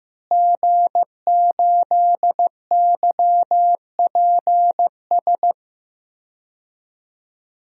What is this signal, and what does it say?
Morse code, heard as a keyed tone, about 15 wpm, 700 Hz. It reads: G8YPS
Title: DXCC-100 Call Signs - 5 Characters - Repeated 2x - 15wpm